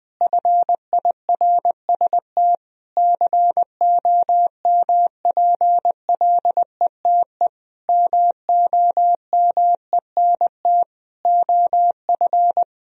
Morse code, heard as a keyed tone, about 20 wpm, 700 Hz. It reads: FIRST COMPLETE MOMENT OF
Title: Sentences from Top 1000 Words - Review Part 2 - 20wpm